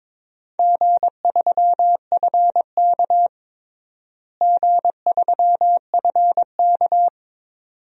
Morse code, heard as a keyed tone, about 22 wpm, 700 Hz. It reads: G3FK G3FK
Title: DXCC-100 Call Signs - 4 Characters - Repeated 2x - 22wpm